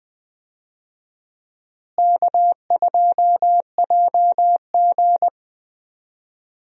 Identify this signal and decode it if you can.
Morse code, heard as a keyed tone, about 20 wpm, 700 Hz. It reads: K2JG
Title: USA Call Signs - 4 Characters - Repeated 2x - 20wpm